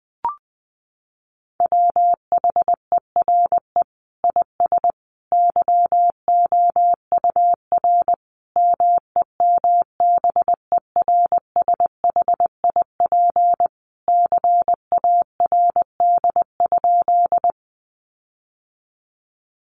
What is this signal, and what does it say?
Morse code, heard as a keyed tone, about 20 wpm, 700 Hz. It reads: WHERE IS YOUR MEMBERSHIP CARD?
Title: Sentences from Top 1400 Words 20wpm